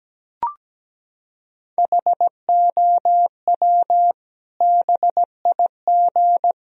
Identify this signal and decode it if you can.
Morse code, heard as a keyed tone, about 17 wpm, 700 Hz. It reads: HOW BIG
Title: Sentences from Top 1000 Words - Review Part 4 - 17wpm